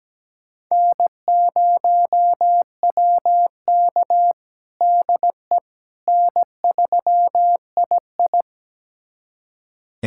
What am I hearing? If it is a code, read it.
Morse code, heard as a keyed tone, about 17 wpm, 700 Hz. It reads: N0WK DE N3II
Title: USA Callsign DE Callsign - 4 Characters - 17wpm